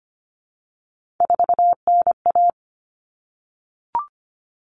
Morse code, heard as a keyed tone, about 25 wpm, 700 Hz. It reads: 4NA
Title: Common Cut Numbers - 3-Digit 25wpm